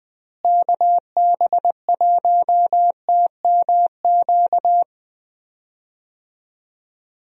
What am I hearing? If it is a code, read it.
Morse code, heard as a keyed tone, about 20 wpm, 700 Hz. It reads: KB1TMQ
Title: USA Call Signs - 6 Characters - Repeated 2x - 20wpm